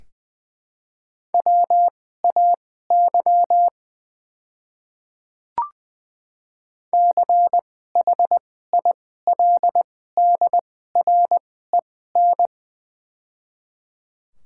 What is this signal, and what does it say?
Morse code, heard as a keyed tone, about 20 wpm, spaced as for 14 wpm, 700 Hz. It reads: WAY CHILDREN
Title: 2x Character Spacing - Top 300 Words - 20wpm